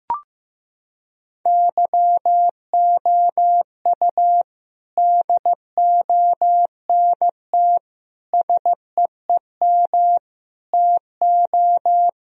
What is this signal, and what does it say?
Morse code, heard as a keyed tone, about 15 wpm, 700 Hz. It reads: YOU DONT SEEM TO
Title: Sentences from Top 1000 Words - Review Part 1 - 15wpm